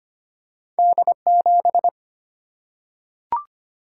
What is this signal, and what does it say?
Morse code, heard as a keyed tone, about 25 wpm, 700 Hz. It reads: D7
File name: Two Letter-Number Combinations - 25wpm